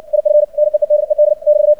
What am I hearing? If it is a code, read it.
Morse code, heard as a keyed tone, about 56 wpm, 600 Hz. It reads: IF NEEDED <BT>
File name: archived / CRSnet